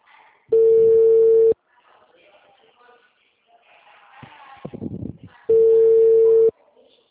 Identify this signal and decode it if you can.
Ringback tone (China, 450 Hz).
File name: out- -2004- - - .70